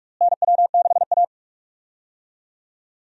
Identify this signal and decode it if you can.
Morse code, heard as a keyed tone, about 46 wpm, 700 Hz. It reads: NW6A
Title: USA Call Signs - 4 Characters - Repeated 2x - 45wpm